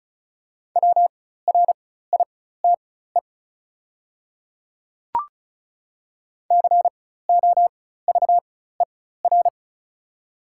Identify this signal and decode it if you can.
Morse code, heard as a keyed tone, about 36 wpm, spaced as for 16 wpm, 700 Hz. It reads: WRITE COVER
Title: 4x Character Spacing - Top 300 Words - 35wpm